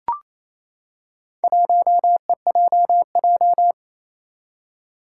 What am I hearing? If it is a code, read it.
Morse code, heard as a keyed tone, about 28 wpm, 700 Hz. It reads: 1EJJ
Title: Four Challenging Letter-Number Combinations 28wpm